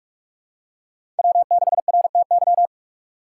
Morse code, heard as a keyed tone, about 46 wpm, 700 Hz. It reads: W6RTY
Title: USA Call Signs - 2x Character Spacing - 45wpm